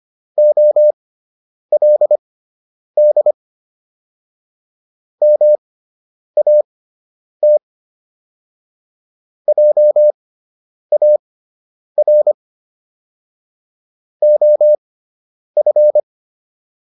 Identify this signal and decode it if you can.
Morse code, heard as a keyed tone, about 25 wpm, spaced as for 9 wpm, 600 Hz. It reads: OLD MAT JAR OF